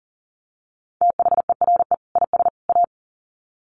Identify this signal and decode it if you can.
Morse code, heard as a keyed tone, about 40 wpm, 700 Hz. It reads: THERE IS A
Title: Sets of 3 Words 40wpm